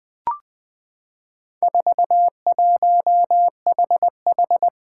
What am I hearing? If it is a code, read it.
Morse code, heard as a keyed tone, about 20 wpm, 700 Hz. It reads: 41HH
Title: Four Challenging Letter-Number Combinations 20wpm